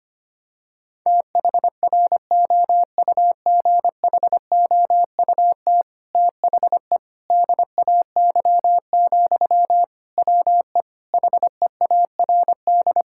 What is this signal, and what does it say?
Morse code, heard as a keyed tone, about 25 wpm, 700 Hz. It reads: THROUGHOUT THE DAY, WE HEARD
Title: Sentences from Top 1000 Words - Review Part 5 - 25wpm